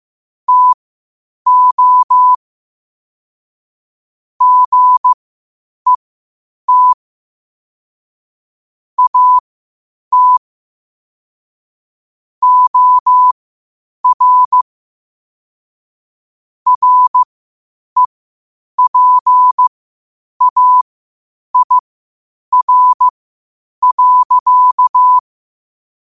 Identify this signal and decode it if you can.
Morse code, heard as a keyed tone, about 15 wpm, spaced as for 8 wpm, 1000 Hz. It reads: TO GET AT OR REPAIR.
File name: qotdfiles / 8wpm